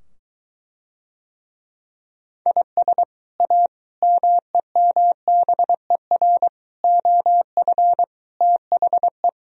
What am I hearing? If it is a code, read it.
Morse code, heard as a keyed tone, about 23 wpm, 700 Hz. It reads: IS A MEMBER OF THE